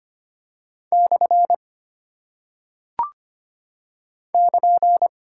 Morse code, heard as a keyed tone, about 25 wpm, 700 Hz. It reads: / <KN>
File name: Single Punctuation-Prosign - 25wpm